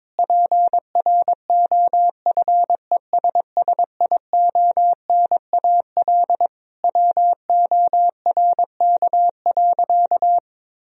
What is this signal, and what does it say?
Morse code, heard as a keyed tone, about 22 wpm, 700 Hz. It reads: PROFESSIONAL WORK.
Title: Sentences from Top 1000 Words - Review Part 4 - 22wpm